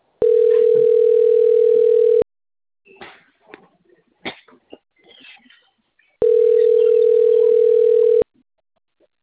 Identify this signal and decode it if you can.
Ringback tone (North America), 440 + 480 Hz.